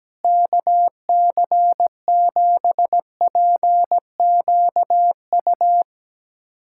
Morse code, heard as a keyed tone, about 17 wpm, 700 Hz. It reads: KC7PQU
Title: USA Callsign DE Callsign - 6 Characters - 17wpm